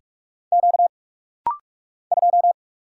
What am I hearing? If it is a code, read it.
Morse code, heard as a keyed tone, about 44 wpm, 690 Hz. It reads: Q 2
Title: Single Letter-Number - Rapid-Fire 45wpm